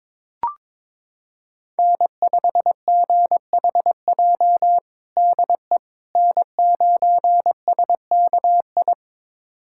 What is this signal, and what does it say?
Morse code, heard as a keyed tone, about 21 wpm, 700 Hz. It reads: N5GHJ DE N9SKI
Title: USA Callsign DE Callsign - 5 Characters - 22wpm